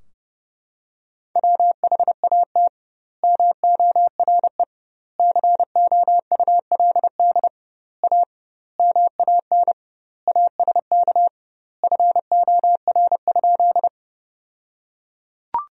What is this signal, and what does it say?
Morse code, heard as a keyed tone, about 30 wpm, 700 Hz. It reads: WHAT MORE COULD A MAN ASK FOR?
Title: 2x Word-Spacing - Sentences from Top 200 Words - 30wpm